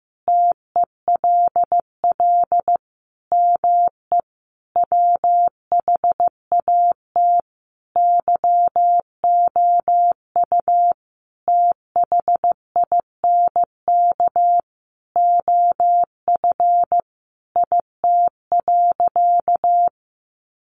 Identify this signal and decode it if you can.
Morse code, heard as a keyed tone, about 15 wpm, 700 Hz. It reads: TELL ME WHAT YOU THINK OF IT.
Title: Sentences from Top 200 Words 15wpm